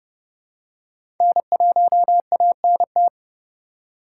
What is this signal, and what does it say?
Morse code, heard as a keyed tone, about 30 wpm, 700 Hz. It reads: N1ANT
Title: USA Call Signs - 5 Characters - 4x Character Spacing - 30wpm